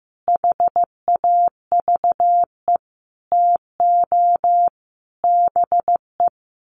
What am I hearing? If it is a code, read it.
Morse code, heard as a keyed tone, about 15 wpm, 700 Hz. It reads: HAVE TO BE
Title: Sets of 5 Words Encore 15wpm